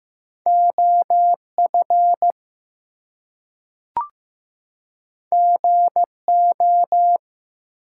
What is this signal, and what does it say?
Morse code, heard as a keyed tone, about 15 wpm, 700 Hz. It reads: OF GO
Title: Two Letter Words - 15wpm